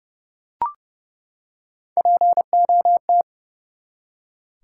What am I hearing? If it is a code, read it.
Morse code, heard as a keyed tone, about 30 wpm, 700 Hz. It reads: POT
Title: Three Letter Words - 30wpm